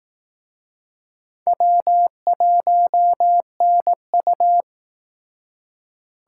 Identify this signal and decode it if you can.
Morse code, heard as a keyed tone, about 19 wpm, 700 Hz. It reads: W1NU